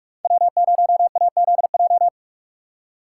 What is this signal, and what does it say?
Morse code, heard as a keyed tone, about 40 wpm, 700 Hz. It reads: W0AZJ